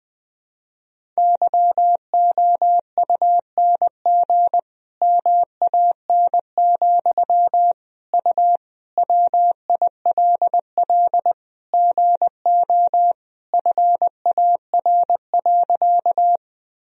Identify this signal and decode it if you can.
Morse code, heard as a keyed tone, about 20 wpm, 700 Hz. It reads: YOUNG MAN, U WILL GO FAR.
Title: Sentences from Top 400 Words and Abbreviations 20wpm